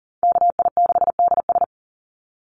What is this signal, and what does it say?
Morse code, heard as a keyed tone, about 42 wpm, 700 Hz. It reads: KI6DS
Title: NA SOTA Chaser Callsigns 40wpm